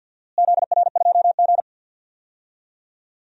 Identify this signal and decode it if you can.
Morse code, heard as a keyed tone, about 52 wpm, 700 Hz. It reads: ZR2G
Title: DXCC-100 Call Signs - 4 Characters - Repeated 3x - 50wpm